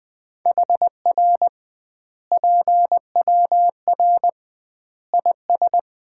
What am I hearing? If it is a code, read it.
Morse code, heard as a keyed tone, about 21 wpm, 700 Hz. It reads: HR PWR IS
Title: Common QSO Phrases - 2x Word-Spacing - 20wpm